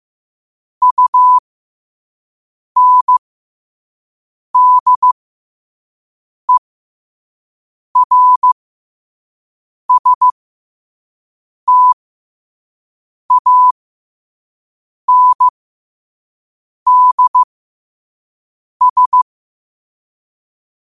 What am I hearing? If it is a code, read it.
Morse code, heard as a keyed tone, about 15 wpm, spaced as for 5 wpm, 1000 Hz. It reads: UNDERSTANDS